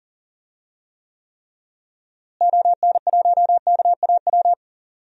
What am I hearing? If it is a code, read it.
Morse code, heard as a keyed tone, about 37 wpm, 700 Hz. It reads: ON1KAW
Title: DXCC-100 Call Signs - 6 Characters - Repeated 3x - 40wpm